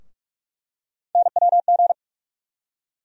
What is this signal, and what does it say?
Morse code, heard as a keyed tone, about 46 wpm, 700 Hz. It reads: NWG
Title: USA Call Sign Postfixes - 3 Letters 45wpm